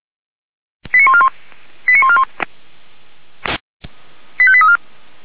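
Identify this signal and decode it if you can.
Five-tone selective call (ZVEI), heard as quick runs of five tones, tones of about 70 ms. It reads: 89141 89141 85834
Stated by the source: ZVEI five-tone selective call received with an SDR, archived by Feuerwehr Orlamuende.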